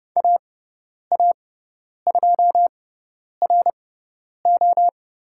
Morse code, heard as a keyed tone, about 30 wpm, spaced as for 10 wpm, 700 Hz. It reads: AA2RO
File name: Lesson 16g - Letter c - Call Signs 30wpm